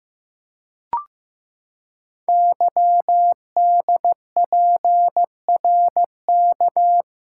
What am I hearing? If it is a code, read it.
Morse code, heard as a keyed tone, about 15 wpm, 700 Hz. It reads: YDPRK
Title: Code Groups - 5 Letters - 15wpm